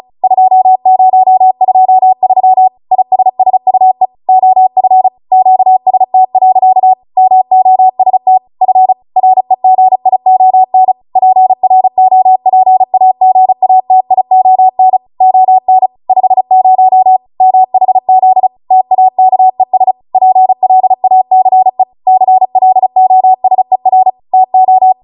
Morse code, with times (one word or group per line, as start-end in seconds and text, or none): 0.2-2.7 2023
2.9-4.1 ISSUE
4.3-5.1 OF
5.3-6.9 QST.
7.2-8.4 MOST
8.6-8.9 F
9.2-10.9 REGION
11.2-15.0 PROPAGATION
15.2-15.9 ON
16.1-17.2 50
17.4-18.5 MHZ
18.7-19.9 TAKES
20.1-21.8 PLACE
22.1-24.1 CLOSER
24.3-24.9 TO